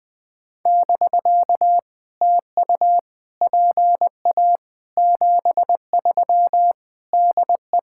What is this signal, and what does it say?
Morse code, heard as a keyed tone, about 21 wpm, 700 Hz. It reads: <BK> TU PA 73 DE
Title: POTA QSO Practice - 20wpm